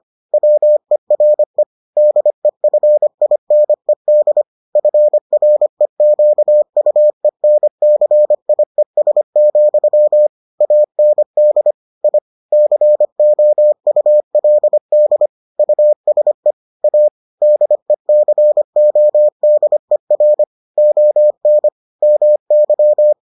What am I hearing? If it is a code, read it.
Morse code, heard as a keyed tone, about 25 wpm, 600 Hz. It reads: WERE DEFINED FREQUENCIES, AND I COULD USE A DECODER ON MY